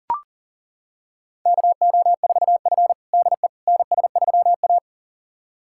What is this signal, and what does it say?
Morse code, heard as a keyed tone, about 40 wpm, 700 Hz. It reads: KO4F DE NS3A
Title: USA Callsign DE Callsign - 4 Characters - 40wpm